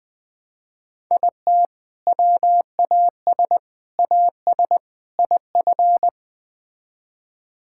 Morse code, heard as a keyed tone, about 20 wpm, 700 Hz. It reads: IT WAS AS IF
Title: Build A Phrase - 4-Word Phrases - 20wpm